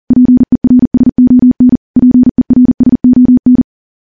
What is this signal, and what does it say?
Morse code, heard as a keyed tone, about 40 wpm, 250 Hz. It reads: PERSON PERSON
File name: morse / morse-training-40WPM